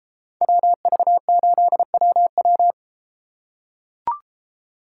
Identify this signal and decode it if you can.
Morse code, heard as a keyed tone, about 32 wpm, 700 Hz. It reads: WV8WW